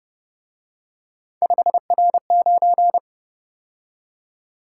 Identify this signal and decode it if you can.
Morse code, heard as a keyed tone, about 29 wpm, 700 Hz. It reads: 5R9